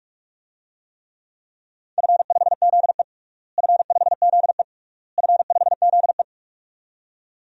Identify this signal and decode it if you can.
Morse code, heard as a keyed tone, about 46 wpm, 700 Hz. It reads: F5ZE F5ZE F5ZE
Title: DXCC-100 Call Signs - 4 Characters - Repeated 3x - 45wpm